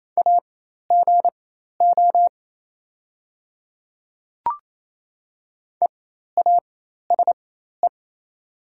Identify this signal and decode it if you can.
Morse code, heard as a keyed tone, about 28 wpm, spaced as for 13 wpm, 700 Hz. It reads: AGO EASE
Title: 4x Character Spacing - Top 500 Words - 28wpm